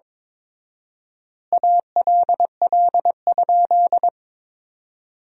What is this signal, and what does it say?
Morse code, heard as a keyed tone, about 22 wpm, 700 Hz. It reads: ALL?
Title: 4x Word-Spacing - Sentences from Top 200 Words - 22wpm